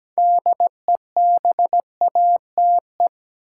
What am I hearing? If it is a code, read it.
Morse code, heard as a keyed tone, about 17 wpm, 700 Hz. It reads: DEBATE
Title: Sentences from Top 1100 Words - New Word Drill - 17wpm